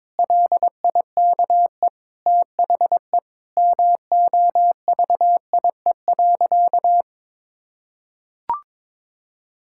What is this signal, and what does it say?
Morse code, heard as a keyed tone, about 22 wpm, 700 Hz. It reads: LIKE THE MOVIE.
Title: Sentences from Top 1000 Words - Review Part 5 - 22wpm